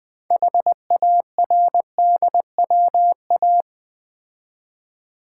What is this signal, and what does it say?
Morse code, heard as a keyed tone, about 20 wpm, 700 Hz. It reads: HARDWA
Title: Build A Word - 2 Letters At A Time - 8-Letter Words - 20wpm